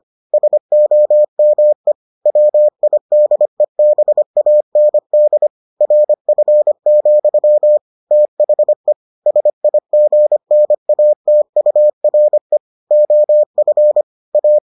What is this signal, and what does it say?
Morse code, heard as a keyed tone, about 25 wpm, 600 Hz. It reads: SOME WIDEBAND RF, THE SIGNATURE OF A